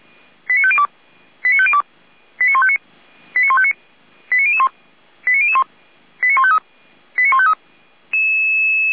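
Five-tone selective call (ZVEI), heard as quick runs of five tones, tones of about 70 ms. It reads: 89502 89502 89159 89159 89001 89001 89153 89153
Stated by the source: ZVEI five-tone call received by SDR, from the feuerwehr-orlamuende.org fire brigade audio server, station f85360.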